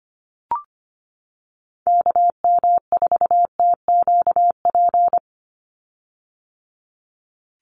Morse code, heard as a keyed tone, about 25 wpm, 700 Hz. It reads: KM4TQP
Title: USA Call Signs 25wpm